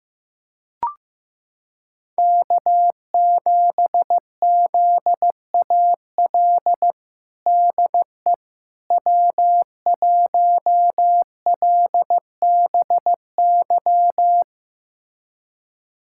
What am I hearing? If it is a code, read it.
Morse code, heard as a keyed tone, about 16 wpm, 700 Hz. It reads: K7ZAL DE W1LBY
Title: USA Callsign DE Callsign - 5 Characters - 15wpm